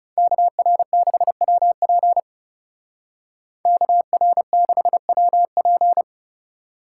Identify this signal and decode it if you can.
Morse code, heard as a keyed tone, about 35 wpm, 700 Hz. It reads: KR6WP KR6WP
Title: Speed Racing - USA 2x2 Call Signs 30wpm